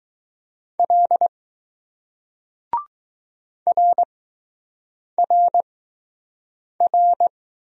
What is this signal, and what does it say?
Morse code, heard as a keyed tone, about 20 wpm, spaced as for 6 wpm, 700 Hz. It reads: L RRR